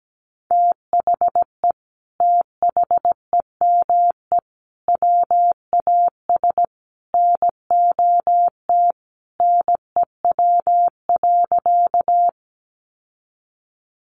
Morse code, heard as a keyed tone, about 17 wpm, 700 Hz. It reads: THE THEME WAS NOT NEW.
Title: Sentences from Top 1100 Words 17wpm